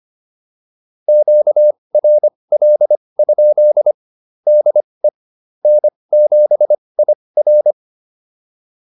Morse code, heard as a keyed tone, about 25 wpm, spaced as for 20 wpm, 600 Hz. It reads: QRL? DE N7IR